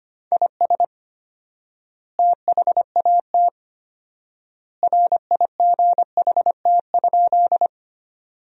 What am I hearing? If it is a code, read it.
Morse code, heard as a keyed tone, about 25 wpm, 700 Hz. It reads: IS THAT RIGHT?
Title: 4x Word-Spacing - Sentences from Top 200 Words - 25wpm